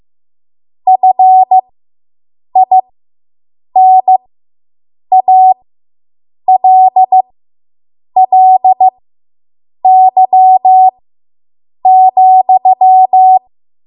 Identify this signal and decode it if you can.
Morse code, heard as a keyed tone, about 15 wpm, spaced as for 7 wpm, 750 Hz. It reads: FINALLY,